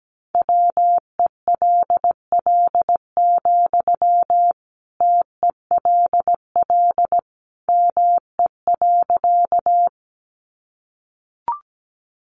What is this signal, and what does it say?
Morse code, heard as a keyed tone, about 17 wpm, 700 Hz. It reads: WELL, TELL ME.
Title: Sentences from Top 200 Words 17wpm